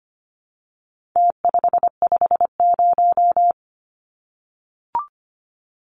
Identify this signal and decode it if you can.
Morse code, heard as a keyed tone, about 25 wpm, 700 Hz. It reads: T550
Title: Numbers - Contest Style - 4-Digit 25wpm